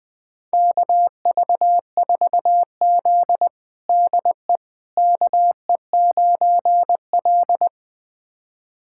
Morse code, heard as a keyed tone, about 20 wpm, 700 Hz. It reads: KV4Z DE KE9L
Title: USA Callsign DE Callsign - 4 Characters - 20wpm